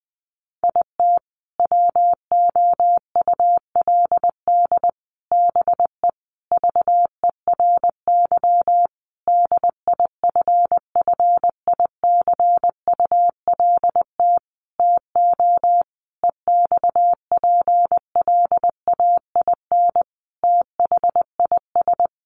Morse code, heard as a keyed tone, about 20 wpm, 700 Hz. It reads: IT WOULD BE VERY DIFFICULT TO EXPLAIN THIS
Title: Sentences from Top 1100 Words 20wpm